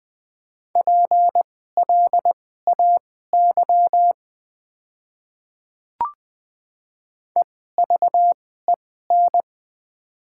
Morse code, heard as a keyed tone, about 20 wpm, spaced as for 14 wpm, 700 Hz. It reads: PLAY EVEN
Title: 2x Character Spacing - Top 300 Words - 20wpm